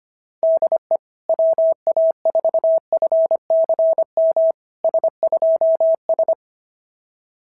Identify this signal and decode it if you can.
Morse code, heard as a keyed tone, about 25 wpm, 690 Hz. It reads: DE WA4FCM S2S
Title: SOTA QSO Practice 25wpm